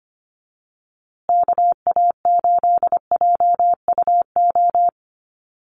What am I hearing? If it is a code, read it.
Morse code, heard as a keyed tone, about 24 wpm, 700 Hz. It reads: KA8JUO
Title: USA Call Signs Repeated 2x 25wpm